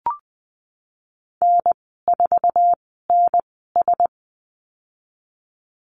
Morse code, heard as a keyed tone, about 20 wpm, spaced as for 15 wpm, 700 Hz. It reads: N4NS
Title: 048 Lesson 9g - Number 4 - Call Signs 20wpm